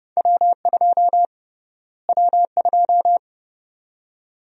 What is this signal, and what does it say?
Morse code, heard as a keyed tone, about 31 wpm, 700 Hz. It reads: W2 W2
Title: USA Call Sign Prefixes - Repeated 2x - 30wpm